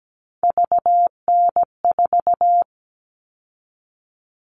Morse code, heard as a keyed tone, about 17 wpm, 700 Hz. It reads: VN4
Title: All Cut Numbers - 3-Digit 17wpm